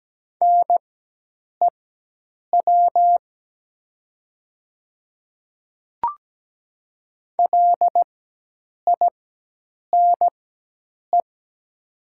Morse code, heard as a keyed tone, about 17 wpm, spaced as for 8 wpm, 700 Hz. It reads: NEW LINE
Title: 4x Character Spacing - Top 200 Words - 17wpm